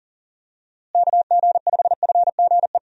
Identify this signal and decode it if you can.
Morse code, heard as a keyed tone, about 40 wpm, 700 Hz. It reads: KG5FGE
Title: USA Call Signs - 6 Characters - Repeated 2x - 40wpm